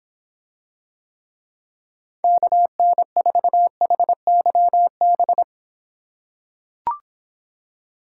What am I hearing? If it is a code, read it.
Morse code, heard as a keyed tone, about 25 wpm, 700 Hz. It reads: KN4HYB